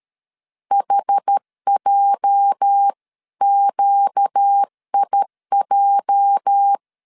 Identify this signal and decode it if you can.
Morse code, heard as a keyed tone, about 13 wpm, 800 Hz. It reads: HJQIJ
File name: LEZIONI / 8 YJ45 / 8 YJ 12wpm 2235